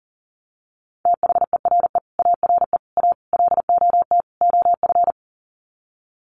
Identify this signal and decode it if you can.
Morse code, heard as a keyed tone, about 40 wpm, 700 Hz. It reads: THERE ARE A LOT OF
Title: Sets of 5 Words 40wpm